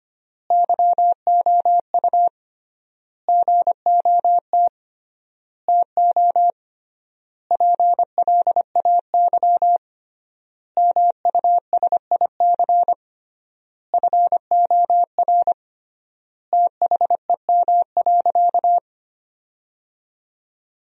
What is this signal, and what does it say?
Morse code, heard as a keyed tone, about 25 wpm, 700 Hz. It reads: YOU GOT TO PLAY MUSIC FOR THEM.
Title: 3x Word-Spacing - Sentences from Top 300 Words - 25wpm